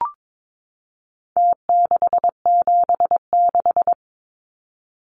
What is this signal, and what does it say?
Morse code, heard as a keyed tone, about 22 wpm, 700 Hz. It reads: T676
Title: Numbers - Contest Style - 4-Digit 22wpm